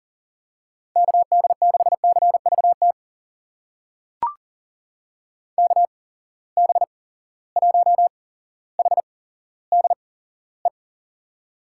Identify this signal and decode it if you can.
Morse code, heard as a keyed tone, about 38 wpm, spaced as for 17 wpm, 700 Hz. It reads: KD6CVT KB1HDE
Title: USA Call Signs - 6 Characters - 8x Character Spacing - 40wpm